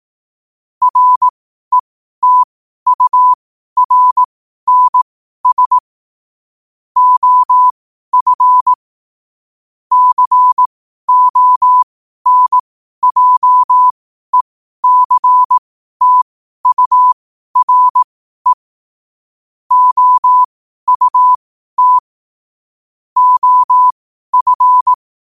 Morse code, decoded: RETURNS OF CONJECTURE OUT OF

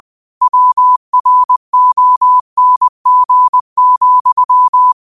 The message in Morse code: WRONG,

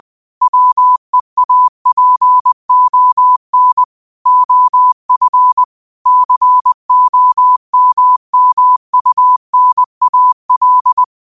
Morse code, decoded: WEAPON OF COMMUNAL

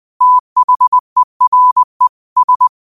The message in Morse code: THERE S

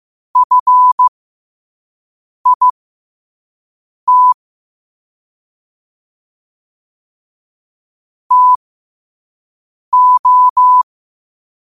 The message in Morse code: FIT TO

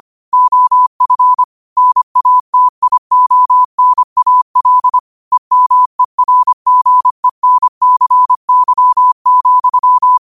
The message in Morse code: OF NATIONAL EMERGENCY,

